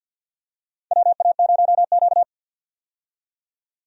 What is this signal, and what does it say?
Morse code, heard as a keyed tone, about 53 wpm, 700 Hz. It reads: WA0Q
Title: USA Call Signs - 4 Characters - 50wpm